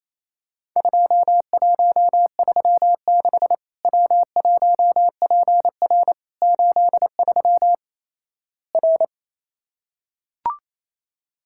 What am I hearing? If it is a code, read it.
Morse code, heard as a keyed tone, about 28 wpm, 700 Hz. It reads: 2136 W1PR 83 R
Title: Worked All Europe DX Contest - QTC Exchange - 28wpm